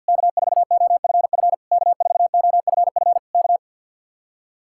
Morse code, heard as a keyed tone, about 64 wpm, 700 Hz. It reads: K4OFF K4OFF K